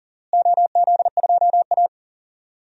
Morse code, heard as a keyed tone, about 40 wpm, 700 Hz. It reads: OZ2A
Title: DXCC-100 Call Signs - 4 Characters - Repeated 3x - 40wpm